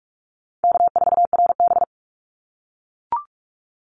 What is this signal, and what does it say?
Morse code, heard as a keyed tone, about 44 wpm, 700 Hz. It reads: K4RB
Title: Easy Call Signs - 45wpm